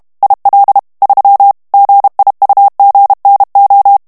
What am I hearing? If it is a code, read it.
Morse code, heard as a keyed tone, about 32 wpm, 800 Hz. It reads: IL 3 GIUGNO
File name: morse mp3 / ITALIEN texte clair 1 vitesse 30 mots minute